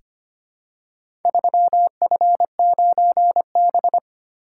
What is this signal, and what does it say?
Morse code, heard as a keyed tone, about 25 wpm, 700 Hz. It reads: 3F9B
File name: practice / 25wpm / Four Letter-Number Combinations - 25wpm